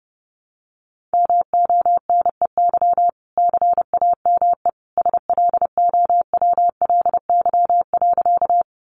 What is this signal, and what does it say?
Morse code, heard as a keyed tone, about 30 wpm, 700 Hz. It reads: MONEY CAME SLOWLY.
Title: Sentences from Top 1100 Words 30wpm